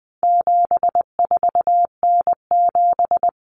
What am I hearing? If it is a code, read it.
Morse code, heard as a keyed tone, about 20 wpm, 700 Hz. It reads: 74N7
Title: Common Cut Numbers - 4-Digit 20wpm